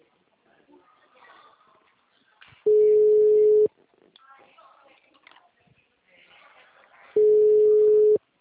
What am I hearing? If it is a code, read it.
Ringback tone (Europe, 425 Hz), with speech.
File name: out- -2008- - - .5834